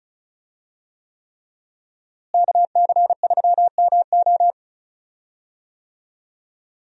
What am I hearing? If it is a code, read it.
Morse code, heard as a keyed tone, about 36 wpm, 700 Hz. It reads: KC3MO
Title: USA Call Signs - 5 Characters - Repeated 3x - 35wpm